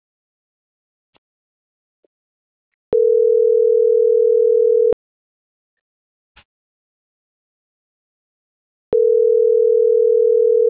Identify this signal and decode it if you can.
Ringback tone (North America), 440 + 480 Hz.